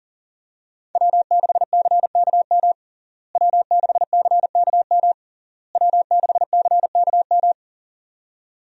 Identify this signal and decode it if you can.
Morse code, heard as a keyed tone, about 40 wpm, 700 Hz. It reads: W6CKM W6CKM W6CKM
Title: USA Call Signs - 5 Characters - Repeated 3x - 40wpm